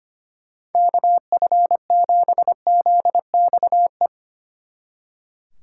Morse code, heard as a keyed tone, about 33 wpm, 700 Hz. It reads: KF7ZXE